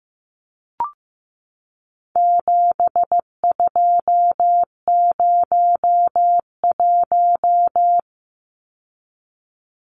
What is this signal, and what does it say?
Morse code, heard as a keyed tone, about 15 wpm, 700 Hz. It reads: 7201
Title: Numbers - 4-Digit 15wpm